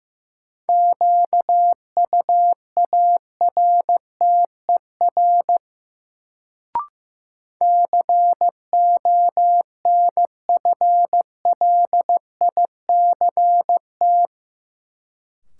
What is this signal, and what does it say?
Morse code, heard as a keyed tone, about 15 wpm, 700 Hz. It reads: QUARTER CONFLICT
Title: Sentences from Top 1300 Words - New Word Drill - 15wpm